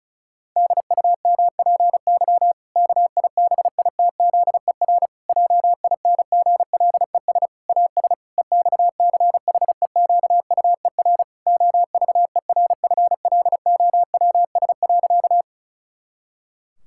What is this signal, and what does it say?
Morse code, heard as a keyed tone, about 36 wpm, 700 Hz. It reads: DUMPY KIBITZER JINGLES AS EXCHEQUER OVERFLOWS.
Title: Pangram Challenge v2 - 35wpm